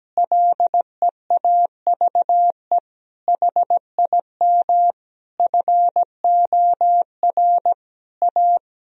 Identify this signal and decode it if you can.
Morse code, heard as a keyed tone, about 17 wpm, 700 Hz. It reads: LEAVE HIM FOR A
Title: Sentences from Top 1000 Words - Review Part 4 - 17wpm